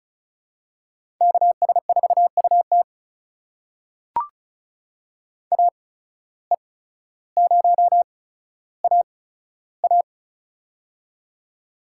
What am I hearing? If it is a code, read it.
Morse code, heard as a keyed tone, about 36 wpm, spaced as for 15 wpm, 700 Hz. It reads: KS4UT AE0AA
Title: USA Call Signs - 5 Characters - 8x Character Spacing - 35wpm